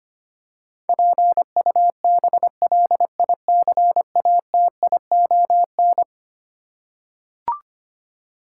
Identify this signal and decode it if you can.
Morse code, heard as a keyed tone, about 25 wpm, 700 Hz. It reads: PUBLICATION
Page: morse-code-ninja-practice-25wpm